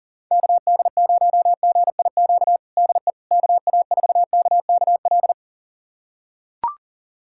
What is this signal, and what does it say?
Morse code, heard as a keyed tone, about 37 wpm, 700 Hz. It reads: KD0GIQ DE KA4KKL
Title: USA Callsign DE Callsign - 6 Characters - 40wpm